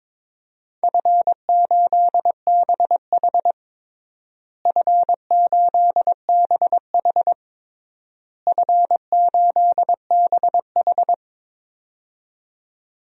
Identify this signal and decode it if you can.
Morse code, heard as a keyed tone, about 22 wpm, 700 Hz. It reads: F8BH F8BH F8BH
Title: DXCC-100 Call Signs - 4 Characters - Repeated 3x - 22wpm